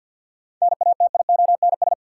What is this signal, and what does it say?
Morse code, heard as a keyed tone, about 50 wpm, 700 Hz. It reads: NATIONS
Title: Sets of 3 Words - New Word Drill - 50wpm